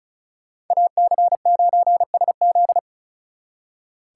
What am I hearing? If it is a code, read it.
Morse code, heard as a keyed tone, about 46 wpm, 700 Hz. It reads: AC9SZ